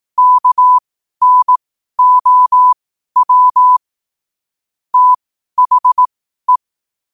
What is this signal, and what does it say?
Morse code, heard as a keyed tone, about 18 wpm, spaced as for 12 wpm, 1000 Hz. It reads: KNOW THE
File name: qotdfiles / 12wpm